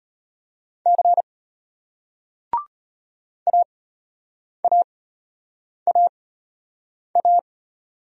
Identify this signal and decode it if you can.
Morse code, heard as a keyed tone, about 36 wpm, spaced as for 8 wpm, 700 Hz. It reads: C AAAA